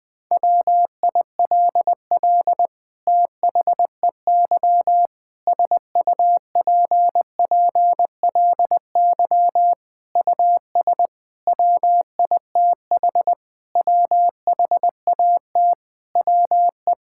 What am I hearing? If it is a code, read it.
Morse code, heard as a keyed tone, about 20 wpm, 700 Hz. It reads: WILL THEY SUPPLY US WITH WHAT WE
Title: Sentences from Top 1000 Words - Review Part 5 - 20wpm